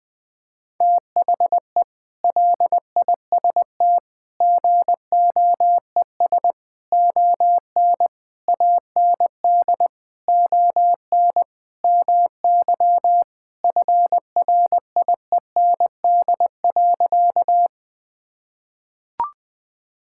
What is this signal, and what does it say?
Morse code, heard as a keyed tone, about 20 wpm, 700 Hz. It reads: THE LIST GOES ON AND ON MY FRIEND.
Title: Sentences with 6-Word Phrases - 20wpm